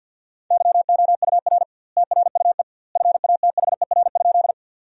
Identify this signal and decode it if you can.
Morse code, heard as a keyed tone, about 47 wpm, 700 Hz. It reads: YOUR TRUE FATHER?
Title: 2x Word-Spacing - Sentences from Top 400 Words - 50wpm